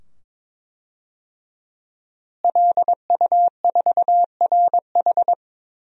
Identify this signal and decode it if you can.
Morse code, heard as a keyed tone, about 22 wpm, 700 Hz. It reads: LU4RH
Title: DXCC-100 Call Signs - 5 Characters - Repeated 2x - 22wpm